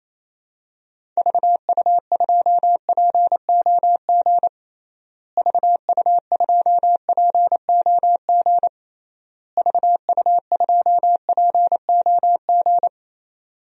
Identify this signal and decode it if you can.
Morse code, heard as a keyed tone, about 28 wpm, 700 Hz. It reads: VU2POG VU2POG VU2POG
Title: DXCC-100 Call Signs - 6 Characters - Repeated 3x - 28wpm